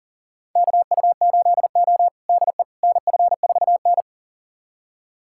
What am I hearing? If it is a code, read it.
Morse code, heard as a keyed tone, about 37 wpm, 700 Hz. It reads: KU8O DE NF4N
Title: USA Callsign DE Callsign - 4 Characters - 40wpm